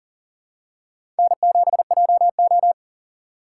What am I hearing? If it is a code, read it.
Morse code, heard as a keyed tone, about 40 wpm, 700 Hz. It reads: N7JO
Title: USA Call Signs - 4 Characters - 3x Character Spacing - 40wpm